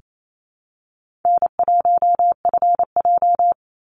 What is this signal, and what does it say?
Morse code, heard as a keyed tone, about 28 wpm, 700 Hz. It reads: N1FJ
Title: NA SOTA Activator Callsigns 28wpm